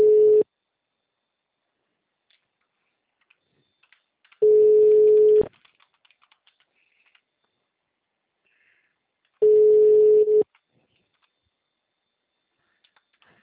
Ringback tone (Europe, 425 Hz).